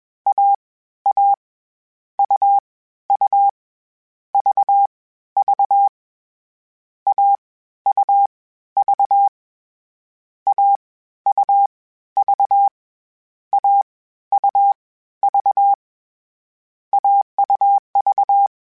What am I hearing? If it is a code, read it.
Morse code, heard as a keyed tone, about 21 wpm, spaced as for 13 wpm, 800 Hz. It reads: AA UU VV AUV AUV AUV AUV
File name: Difficult Combos / Difficult Combos-mp3s / AUV ShortRun @20